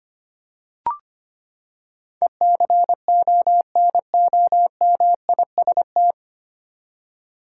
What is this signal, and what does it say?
Morse code, heard as a keyed tone, about 25 wpm, 700 Hz. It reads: ECONOMIST
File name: Nine Letter Words - 25wpm